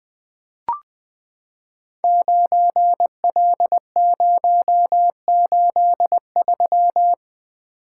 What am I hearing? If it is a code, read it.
Morse code, heard as a keyed tone, about 20 wpm, 700 Hz. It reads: 9L083
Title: Code Groups - 5 Letter:Numbers - 20wpm